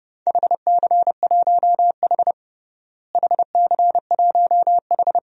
Morse code, heard as a keyed tone, about 30 wpm, 700 Hz. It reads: HC1H HC1H
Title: DXCC-100 Call Signs - 4 Characters - Repeated 2x - 30wpm